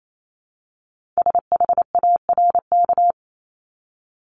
Morse code, heard as a keyed tone, about 28 wpm, 700 Hz. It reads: SHARK
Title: Multi-letter Phonemes 28wpm